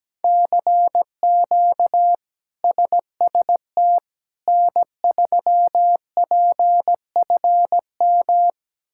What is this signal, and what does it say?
Morse code, heard as a keyed tone, about 17 wpm, 700 Hz. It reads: CQ SST N3PFM
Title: K1USN SST QSO Practice - 17wpm